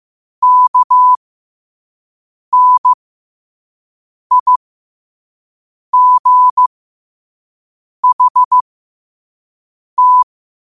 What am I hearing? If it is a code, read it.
Morse code, heard as a keyed tone, about 15 wpm, spaced as for 5 wpm, 1000 Hz. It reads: KNIGHT